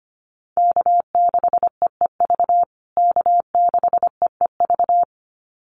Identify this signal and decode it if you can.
Morse code, heard as a keyed tone, about 25 wpm, 700 Hz. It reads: K6EEV K6EEV
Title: USA Call Signs Repeated 2x 25wpm